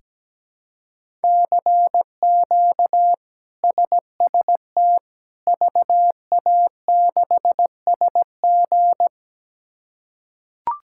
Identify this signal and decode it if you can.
Morse code, heard as a keyed tone, about 17 wpm, 700 Hz. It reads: CQ SST VA6SG
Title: K1USN SST QSO Practice - 17wpm